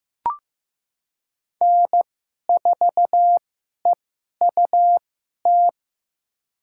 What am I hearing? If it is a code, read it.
Morse code, heard as a keyed tone, about 16 wpm, spaced as for 11 wpm, 700 Hz. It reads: N4EUT
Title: USA Call Signs - 5 Characters - 2x Character Spacing - 15wpm